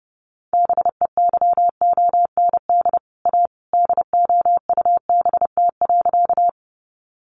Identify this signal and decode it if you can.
Morse code, heard as a keyed tone, about 30 wpm, 700 Hz. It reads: BEYOND A DOUBT.
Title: Sentences from Top 1500 Words 30wpm